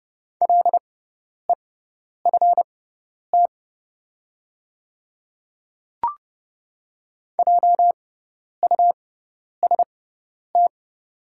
Morse code, heard as a keyed tone, about 30 wpm, spaced as for 10 wpm, 700 Hz. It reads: LEFT JUST
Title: 6x Character Spacing - Top 500 Words - 30wpm